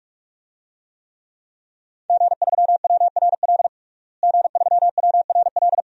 Morse code, heard as a keyed tone, about 46 wpm, 700 Hz. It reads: G3WRL G3WRL
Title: DXCC-100 Call Signs - 5 Characters - Repeated 2x - 45wpm